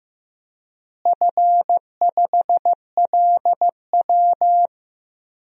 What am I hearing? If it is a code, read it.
Morse code, heard as a keyed tone, about 15 wpm, 700 Hz. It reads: F5LW
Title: DXCC-100 Call Signs - 4 Characters - Repeated 3x - 15wpm